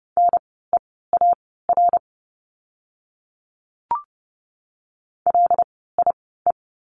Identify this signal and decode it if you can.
Morse code, heard as a keyed tone, about 30 wpm, spaced as for 17 wpm, 700 Hz. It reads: NEAR LIE
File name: Lesson 16f - Letter c - Words 30wpm